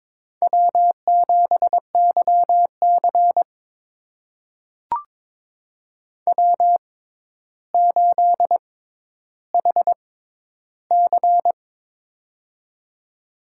Morse code, heard as a keyed tone, about 22 wpm, spaced as for 11 wpm, 700 Hz. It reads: W7YC W8HC
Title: USA Call Signs - 4 Characters - 6x Character Spacing - 22wpm